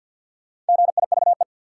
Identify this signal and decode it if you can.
Morse code, heard as a keyed tone, about 47 wpm, 700 Hz. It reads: GIVE
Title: Sets of 2 Words - New Word Drill - 50wpm